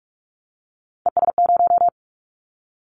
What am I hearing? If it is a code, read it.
Morse code, heard as a keyed tone, about 44 wpm, 700 Hz. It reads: ES0
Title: Callsign Prefixes for Top 50 DXCC - 45wpm